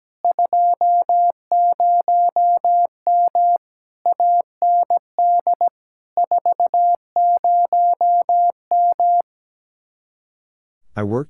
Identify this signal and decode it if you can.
Morse code, heard as a keyed tone, about 17 wpm, 700 Hz. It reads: 20M AND 40M
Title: Speed-Racing Common QSO Phrases - 17wpm